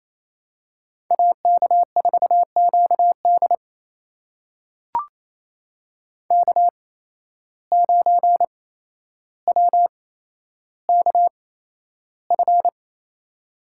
Morse code, heard as a keyed tone, about 28 wpm, spaced as for 12 wpm, 700 Hz. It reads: AK4QD K9WKF
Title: USA Call Signs - 5 Characters - 8x Character Spacing - 28wpm